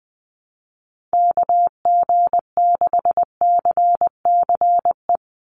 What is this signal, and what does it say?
Morse code, heard as a keyed tone, about 19 wpm, 700 Hz. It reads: KG6CCE
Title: USA Call Signs 20wpm